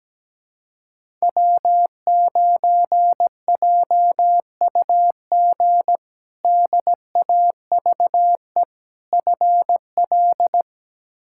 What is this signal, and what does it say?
Morse code, heard as a keyed tone, about 17 wpm, 700 Hz. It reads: W9JUG DAVE FL
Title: K1USN SST QSO Practice - 17wpm